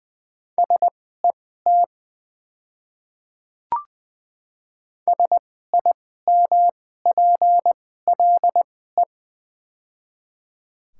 Morse code, heard as a keyed tone, about 20 wpm, spaced as for 14 wpm, 700 Hz. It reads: SET SIMPLE
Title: 2x Character Spacing - Top 500 Words - 20wpm